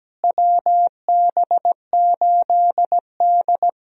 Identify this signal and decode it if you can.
Morse code, heard as a keyed tone, about 17 wpm, 700 Hz. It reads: WB8D
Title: USA Call Signs - 4 Characters - Repeated 3x - 17wpm